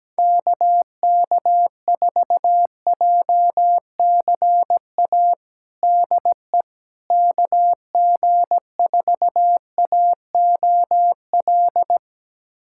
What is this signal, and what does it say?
Morse code, heard as a keyed tone, about 17 wpm, 700 Hz. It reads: KK4JCA DE KG4AOL
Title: USA Callsign DE Callsign - 6 Characters - 17wpm